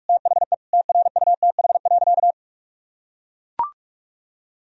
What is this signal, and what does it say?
Morse code, heard as a keyed tone, about 41 wpm, 700 Hz. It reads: THE TRUTH.
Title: Sentences from Top 1000 Words - Review Part 1 - 45wpm